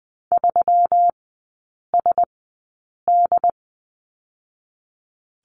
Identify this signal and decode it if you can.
Morse code, heard as a keyed tone, about 20 wpm, 700 Hz. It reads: 3 S D
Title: 121 Lesson 20e - Number 3 - Three Characters 20wpm